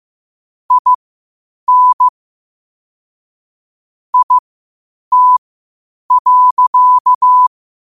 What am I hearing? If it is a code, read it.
Morse code, heard as a keyed tone, about 15 wpm, spaced as for 8 wpm, 1000 Hz. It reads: IN IT.
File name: qotdfiles / 8wpm